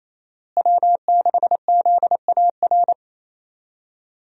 Morse code, heard as a keyed tone, about 28 wpm, 700 Hz. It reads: W6ZAR
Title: USA Call Signs - 5 Characters - Repeated 2x - 28wpm